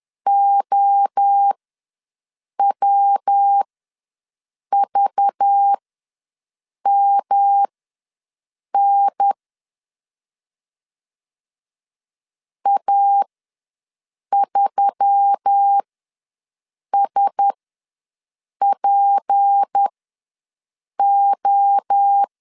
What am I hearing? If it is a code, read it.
Morse code, heard as a keyed tone, about 10 wpm, spaced as for 6 wpm, 800 Hz. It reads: OWVMN A3SPO
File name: LEZIONI / 6 PKX0 / 6 PKX0 ws 10wpm 4220